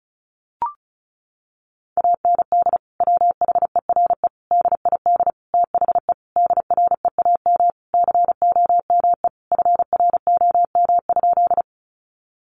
Morse code, heard as a keyed tone, about 34 wpm, 700 Hz. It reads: AND WHERE DID THE DREAM COME FROM?
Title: Sentences from Top 1500 Words 35wpm